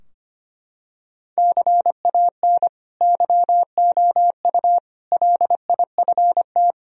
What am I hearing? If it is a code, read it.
Morse code, heard as a keyed tone, about 25 wpm, 700 Hz. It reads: CAN YOU LIFT
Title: Sentences from Top 1000 Words - Review Part 3 - 25wpm